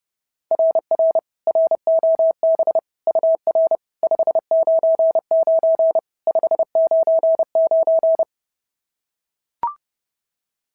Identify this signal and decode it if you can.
Morse code, heard as a keyed tone, about 30 wpm, 660 Hz. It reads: RR ROB UR 599 599
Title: SOTA QSO Practice 30wpm